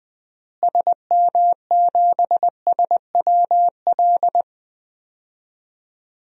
Morse code, heard as a keyed tone, about 21 wpm, 700 Hz. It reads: SM7SWL
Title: DXCC-100 Call Signs - 6 Characters - Repeated 2x - 20wpm